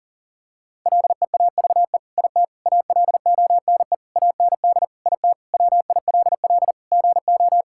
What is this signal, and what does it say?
Morse code, heard as a keyed tone, about 40 wpm, 700 Hz. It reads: LEAVE IT ALONE AND IT WILL GO
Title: Sentences from Top 1000 Words - Review Part 5 - 40wpm